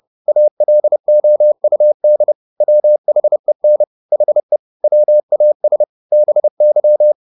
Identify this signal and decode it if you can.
Morse code, heard as a keyed tone, about 30 wpm, 600 Hz. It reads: ALOUD WHEN HE WAS BY
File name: the old man and the sea / the old man and the sea 1 30wpm